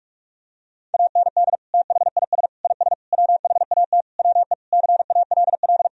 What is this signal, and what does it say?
Morse code, heard as a keyed tone, about 46 wpm, 700 Hz. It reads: AND THIS IS WHAT WE CALL
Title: Sentences from Top 1000 Words - Review Part 3 - 45wpm